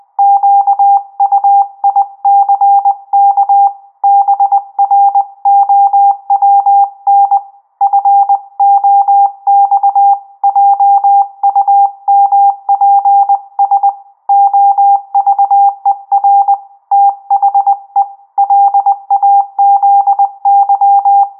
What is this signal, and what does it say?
Morse code, heard as a keyed tone, about 20 wpm, 800 Hz. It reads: QUICK BROWN FOX JUMPS OVER THE LAZY
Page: Trasmissione Morse realizzata automaticamente